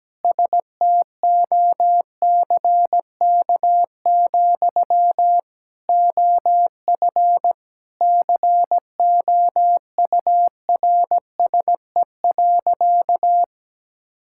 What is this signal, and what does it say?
Morse code, heard as a keyed tone, about 17 wpm, 700 Hz. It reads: STOCK, OF COURSE.
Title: Sentences from Top 1000 Words - Review Part 5 - 17wpm